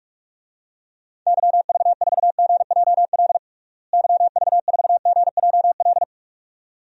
Morse code, heard as a keyed tone, about 41 wpm, 700 Hz. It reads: YV4GJL YV4GJL
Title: DXCC-100 Call Signs - 6 Characters - Repeated 2x - 45wpm